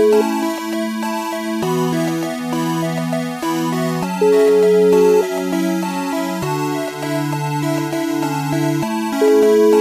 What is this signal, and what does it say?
Ringback tone (Europe, 425 Hz), with noise.